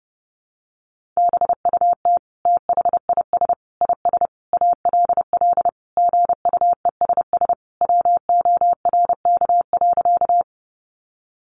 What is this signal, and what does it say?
Morse code, heard as a keyed tone, about 30 wpm, 700 Hz. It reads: BUT THIS IS ALL GUESS WORK.
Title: Sentences from Top 1500 Words 30wpm